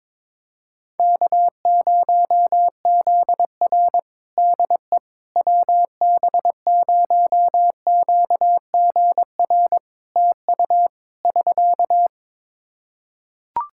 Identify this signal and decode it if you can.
Morse code, heard as a keyed tone, about 21 wpm, 700 Hz. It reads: K0ZR DE WB0QGR TU <SK>
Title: Practice QSO - Exchange 4 - Ending - 22wpm